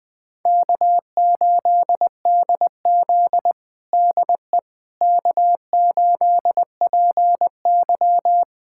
Morse code, heard as a keyed tone, about 20 wpm, 700 Hz. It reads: K8DZ DE K8PY
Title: USA Callsign DE Callsign - 4 Characters - 20wpm